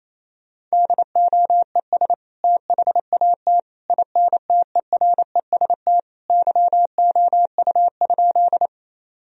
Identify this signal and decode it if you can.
Morse code, heard as a keyed tone, about 28 wpm, 700 Hz. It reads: DOES THAT INTEREST YOU?
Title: Sentences from Top 1000 Words - Review Part 2 - 28wpm